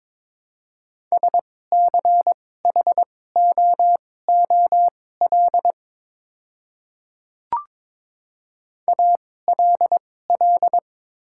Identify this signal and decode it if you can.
Morse code, heard as a keyed tone, about 22 wpm, spaced as for 16 wpm, 700 Hz. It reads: SCHOOL ALL
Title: 2x Character Spacing - Top 300 Words - 22wpm